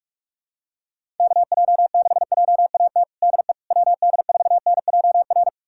Morse code, heard as a keyed tone, about 42 wpm, 700 Hz. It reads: KJ6JAT DE WD4NJR
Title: USA Callsign DE Callsign - 6 Characters - 45wpm